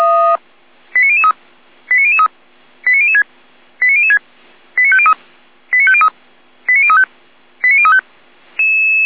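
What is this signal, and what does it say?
Five-tone selective call (ZVEI), heard as quick runs of five tones, tones of about 70 ms. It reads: 89003 89003 89006 89006 89503 89503 89035 89035
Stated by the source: ZVEI five-tone call received by SDR.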